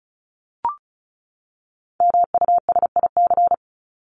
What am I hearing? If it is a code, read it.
Morse code, heard as a keyed tone, about 34 wpm, 700 Hz. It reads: MUSIC
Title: Top 300 Words 35wpm